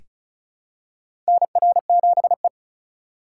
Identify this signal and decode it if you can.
Morse code, heard as a keyed tone, about 36 wpm, 700 Hz. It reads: NR7E
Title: USA Call Signs - 4x Character Spacing - 35wpm